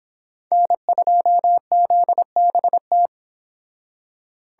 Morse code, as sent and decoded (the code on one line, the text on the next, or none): -. ..--- --.. -... -
N2ZBT